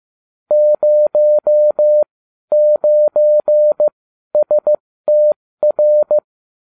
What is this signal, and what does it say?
Morse code, heard as a keyed tone, about 15 wpm, spaced as for 12 wpm, 600 Hz. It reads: 09STR